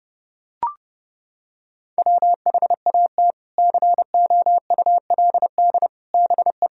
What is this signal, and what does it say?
Morse code, heard as a keyed tone, about 32 wpm, 700 Hz. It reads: WHAT COULD BE
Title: Sentences from Top 1000 Words - Review Part 3 - 30wpm